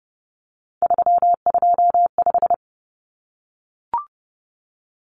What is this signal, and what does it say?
Morse code, heard as a keyed tone, about 30 wpm, 700 Hz. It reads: 325
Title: Numbers - 3-Digit 30wpm